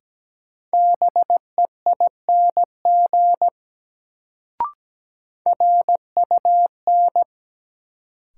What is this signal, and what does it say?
Morse code, heard as a keyed tone, about 17 wpm, 700 Hz. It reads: BEING RUN
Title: Sets of 4 Words - New Word Drill - 17wpm